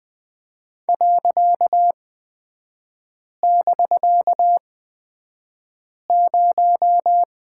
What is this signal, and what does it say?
Morse code, heard as a keyed tone, about 20 wpm, spaced as for 5 wpm, 700 Hz. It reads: .<BK>0
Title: 255 Lesson 41e - Period - Three Characters 20wpm